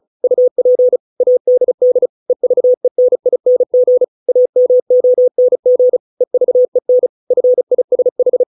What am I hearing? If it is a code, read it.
Morse code, heard as a keyed tone, about 34 wpm, 500 Hz. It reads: UP ADD EVENING AMONG EVEN FISH